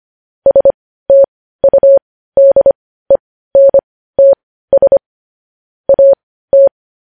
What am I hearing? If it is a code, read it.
Morse code, heard as a keyed tone, about 25 wpm, spaced as for 15 wpm, 550 Hz. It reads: STUDENTS AT
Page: news-headlines-in-morse-code-at-15-wpm